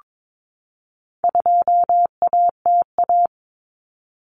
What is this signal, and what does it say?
Morse code, heard as a keyed tone, about 22 wpm, 700 Hz. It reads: 2ATA
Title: Common Cut Numbers - 4-Digit 22wpm